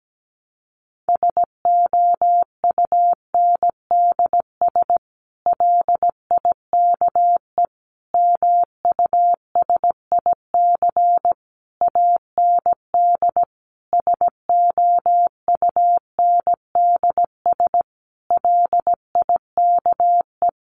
Morse code, heard as a keyed tone, about 17 wpm, 700 Hz. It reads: SOUNDS LIKE MUSIC AND SOUNDS LIKE
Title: A Sound Of Thunder 17wpm